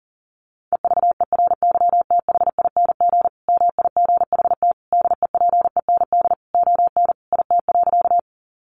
Morse code, heard as a keyed tone, about 40 wpm, 700 Hz. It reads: EVERYTHING MIGHT DEPEND ON IT.
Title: Sentences from Top 1300 Words 40wpm